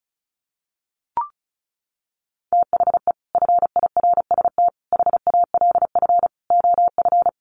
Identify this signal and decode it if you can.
Morse code, heard as a keyed tone, about 34 wpm, 700 Hz. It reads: THE FIRST HALF OF
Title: Sets of 5 Words 35wpm